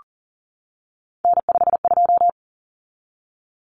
Morse code, heard as a keyed tone, about 40 wpm, 700 Hz. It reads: N52
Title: Common Cut Numbers - 3-Digit 40wpm